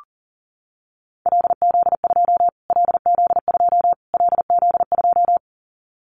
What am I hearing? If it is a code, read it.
Morse code, heard as a keyed tone, about 40 wpm, 700 Hz. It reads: LZ2 LZ2 LZ2
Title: Callsign Prefixes for Top 25 DXCC - Repeated 3x - 40wpm